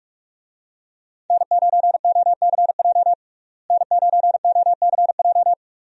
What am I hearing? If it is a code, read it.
Morse code, heard as a keyed tone, about 46 wpm, 700 Hz. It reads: N9OCJ N9OCJ
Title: USA Call Signs - 5 Characters - Repeated 3x - 45wpm